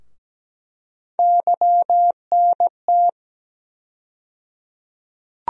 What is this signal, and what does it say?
Morse code, heard as a keyed tone, about 17 wpm, 700 Hz. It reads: YNT
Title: USA Call Sign Postfixes - 3 Letters 17wpm